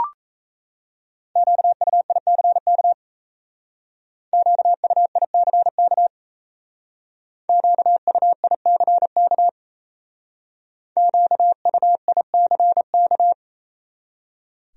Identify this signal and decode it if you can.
Morse code, heard as a keyed tone, about 35 wpm, 700 Hz. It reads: QUICK QUICK QUICK QUICK